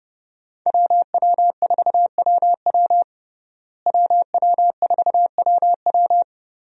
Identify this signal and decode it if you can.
Morse code, heard as a keyed tone, about 30 wpm, 700 Hz. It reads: WW4WW WW4WW
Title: USA Call Signs - 5 Characters - Repeated 2x - 30wpm